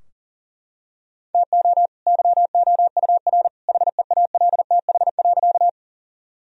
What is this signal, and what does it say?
Morse code, heard as a keyed tone, about 40 wpm, 700 Hz. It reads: TO YOUR HEALTH.
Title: Sentences from Top 1000 Words - Review Part 3 - 40wpm